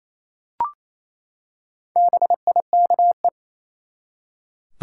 Morse code, heard as a keyed tone, about 28 wpm, 700 Hz. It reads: BIKE
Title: Four Letter Words - 28wpm